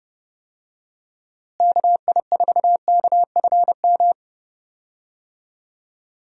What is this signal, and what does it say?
Morse code, heard as a keyed tone, about 30 wpm, 700 Hz. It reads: KI4KFM
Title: USA Call Signs - 6 Characters - Repeated 3x - 30wpm